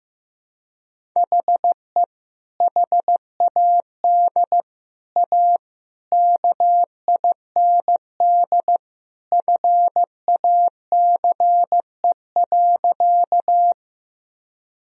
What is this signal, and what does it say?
Morse code, heard as a keyed tone, about 16 wpm, 700 Hz. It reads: HE HAD A KIND FACE.
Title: Sentences from Top 1000 Words - Review Part 2 - 15wpm